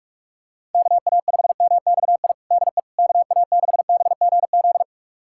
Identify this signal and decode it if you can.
Morse code, heard as a keyed tone, about 46 wpm, 700 Hz. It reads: KA5MXI DE KA6BGZ
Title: USA Callsign DE Callsign - 6 Characters - 45wpm